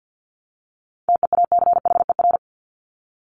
Morse code, heard as a keyed tone, about 50 wpm, 700 Hz. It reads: TEACHER
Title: Top 751-1000 Words 50wpm